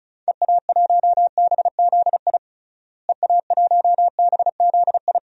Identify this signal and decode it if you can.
Morse code, heard as a keyed tone, about 34 wpm, 700 Hz. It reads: EA1BZI EA1BZI
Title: DXCC-100 Call Signs - 6 Characters - Repeated 3x - 35wpm